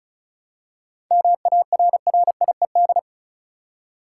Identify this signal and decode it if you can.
Morse code, heard as a keyed tone, about 34 wpm, 700 Hz. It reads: MARRIED
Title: Seven Letter Words - 35wpm